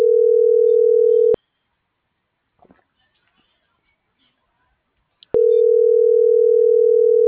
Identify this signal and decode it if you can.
Ringback tone (North America), 440 + 480 Hz.